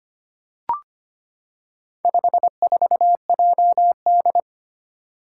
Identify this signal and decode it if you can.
Morse code, heard as a keyed tone, about 25 wpm, 700 Hz. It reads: 54JD
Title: Four Challenging Letter-Number Combinations 25wpm